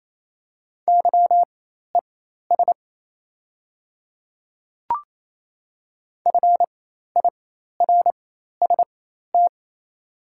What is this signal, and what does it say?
Morse code, heard as a keyed tone, about 28 wpm, spaced as for 13 wpm, 700 Hz. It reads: YES FIRST
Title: 4x Character Spacing - Top 500 Words - 28wpm